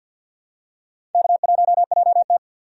Morse code, heard as a keyed tone, about 52 wpm, 700 Hz. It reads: K1JT
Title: USA Call Signs - 4 Characters - 50wpm